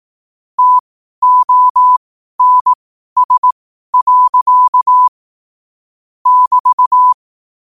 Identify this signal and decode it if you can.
Morse code, heard as a keyed tone, about 18 wpm, spaced as for 12 wpm, 1000 Hz. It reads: TONS. <BT>
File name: qotdfiles / 12wpm